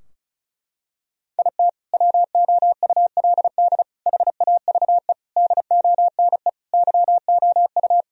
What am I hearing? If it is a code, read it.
Morse code, heard as a keyed tone, about 36 wpm, 700 Hz. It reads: IT WOULD HAVE DONE YOU
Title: Sentences from Top 1000 Words - Review Part 2 - 35wpm